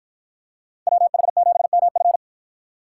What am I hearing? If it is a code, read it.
Morse code, heard as a keyed tone, about 52 wpm, 700 Hz. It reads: WH7MF